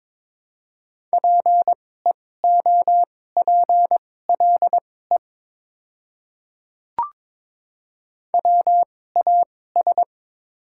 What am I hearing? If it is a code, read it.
Morse code, heard as a keyed tone, about 21 wpm, spaced as for 16 wpm, 700 Hz. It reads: PEOPLE WAS
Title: 2x Character Spacing - Top 100 Words - 22wpm